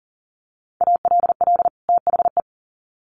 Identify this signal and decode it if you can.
Morse code, heard as a keyed tone, about 40 wpm, 700 Hz. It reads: ALL THE
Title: Sets of 2 Words 40wpm